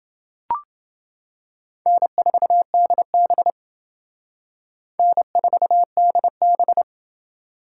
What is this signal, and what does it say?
Morse code, heard as a keyed tone, about 28 wpm, 700 Hz. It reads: N4DB N4DB